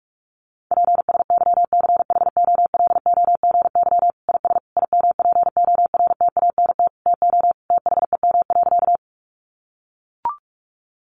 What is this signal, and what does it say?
Morse code, heard as a keyed tone, about 44 wpm, 700 Hz. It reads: PSYCHOLOGY IS IMPORTANT TO THEM.
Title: Sentences from Top 1400 Words 45wpm